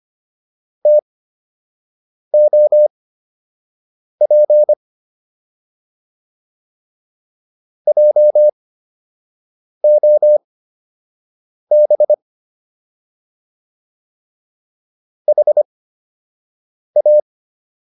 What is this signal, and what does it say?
Morse code, heard as a keyed tone, about 25 wpm, spaced as for 6 wpm, 600 Hz. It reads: TOP JOB HA